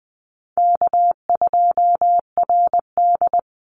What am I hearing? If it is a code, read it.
Morse code, heard as a keyed tone, about 20 wpm, 700 Hz. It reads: K2RD
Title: Call Signs Repeated 3x - 20wpm